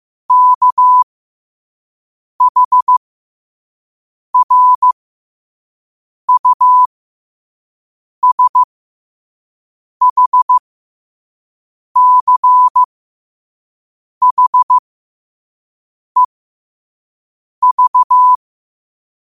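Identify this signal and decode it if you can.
Morse code, heard as a keyed tone, about 15 wpm, spaced as for 5 wpm, 1000 Hz. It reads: KHRUSHCHEV
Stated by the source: Morse at 5 wpm.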